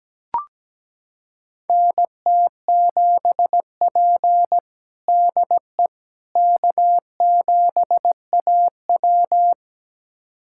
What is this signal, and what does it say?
Morse code, heard as a keyed tone, about 17 wpm, 700 Hz. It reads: NT7P DE K7AW
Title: USA Callsign DE Callsign - 4 Characters - 17wpm